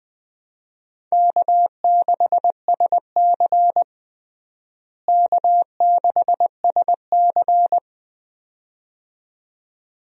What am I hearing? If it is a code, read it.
Morse code, heard as a keyed tone, about 20 wpm, 700 Hz. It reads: K6SC K6SC
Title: USA Call Signs - 4 Characters - Repeated 2x - 20wpm